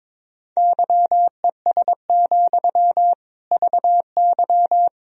Morse code, heard as a keyed tone, about 22 wpm, 700 Hz. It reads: YES, VY
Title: Sentences from Top 500 Words and Abbreviations 22wpm